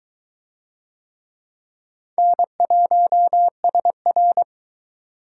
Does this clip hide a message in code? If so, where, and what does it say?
Yes: Morse code from 2.2 to 4.4 s, heard as a keyed tone, about 20 wpm, 700 Hz. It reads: N1SR